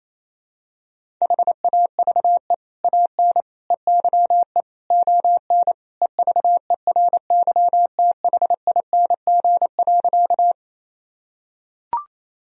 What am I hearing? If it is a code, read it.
Morse code, heard as a keyed tone, about 28 wpm, 700 Hz. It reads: HAVE AN EYE ON EVERYTHING.
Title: Sentences from Top 1000 Words - Review Part 1 - 28wpm